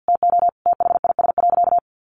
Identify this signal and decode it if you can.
Morse code, heard as a keyed tone, about 50 wpm, 700 Hz. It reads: TO THIS.
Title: Sentences from Top 1400 Words 50wpm